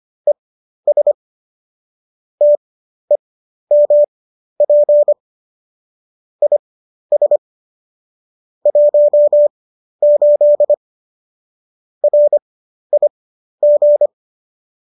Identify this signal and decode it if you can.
Morse code, heard as a keyed tone, about 25 wpm, spaced as for 12 wpm, 590 Hz. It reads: ES TEMP IS 18 RIG